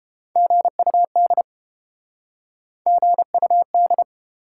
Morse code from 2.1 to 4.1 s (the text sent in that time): GUD